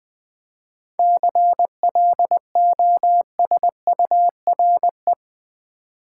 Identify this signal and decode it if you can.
Morse code, heard as a keyed tone, about 20 wpm, 700 Hz. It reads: CLOSURE
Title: 7 Letter Words - First 3 Letters Given - 20wpm